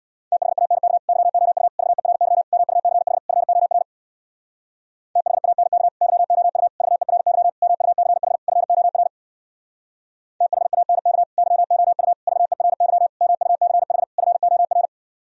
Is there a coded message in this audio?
Yes: Morse code from 0.3 to 14.9 s, heard as a keyed tone, about 64 wpm, 700 Hz. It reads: THANK YOU VERY MUCH FOR THANK YOU VERY MUCH FOR THANK YOU VERY MUCH FOR